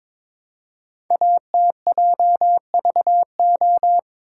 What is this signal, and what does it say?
Morse code, heard as a keyed tone, about 22 wpm, 700 Hz. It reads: ATJVO
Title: Code Groups - 5 Letters - 22wpm